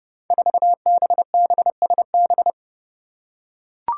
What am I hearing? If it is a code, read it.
Morse code, heard as a keyed tone, about 30 wpm, 700 Hz. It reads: 4BBSB
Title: Five Challenging Letter-Number Combinations 30wpm